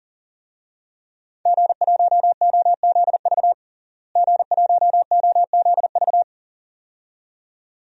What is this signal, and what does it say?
Morse code, heard as a keyed tone, about 37 wpm, 700 Hz. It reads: G1OZV G1OZV
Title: DXCC-100 Call Signs - 5 Characters - Repeated 2x - 40wpm